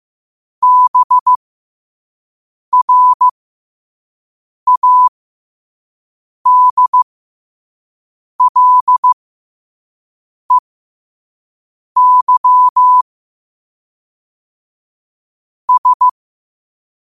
Morse code, heard as a keyed tone, about 15 wpm, spaced as for 5 wpm, 1000 Hz. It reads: BRADLEY S
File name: qotdfiles / 5wpm